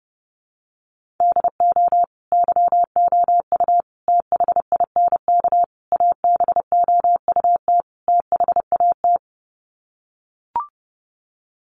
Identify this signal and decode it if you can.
Morse code, heard as a keyed tone, about 30 wpm, 700 Hz. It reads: DO YOU THINK ABOUT THAT
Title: Sets of 5 Words Encore2 30wpm